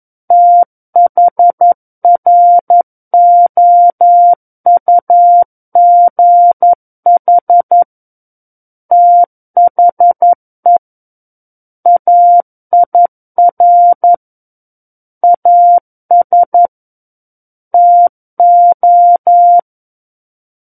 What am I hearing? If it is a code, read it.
Morse code, heard as a keyed tone, about 11 wpm, 700 Hz. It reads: THROUGH THE AIR AS TO
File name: Book One / 02-Chapter 2 - 11 WPM